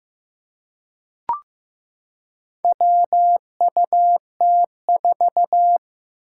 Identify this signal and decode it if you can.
Morse code, heard as a keyed tone, about 15 wpm, 700 Hz. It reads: WUT4
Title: Four Challenging Letter-Number Combinations 15wpm